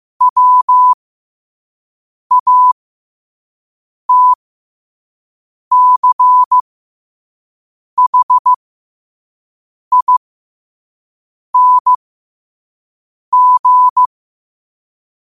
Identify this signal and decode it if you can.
Morse code, heard as a keyed tone, about 15 wpm, spaced as for 5 wpm, 1000 Hz. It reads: WATCHING
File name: qotdfiles / 5wpm